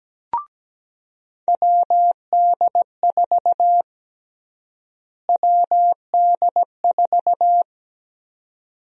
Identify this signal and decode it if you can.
Morse code, heard as a keyed tone, about 17 wpm, 700 Hz. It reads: WD4 WD4
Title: USA Call Sign Prefixes - Repeated 2x - 17wpm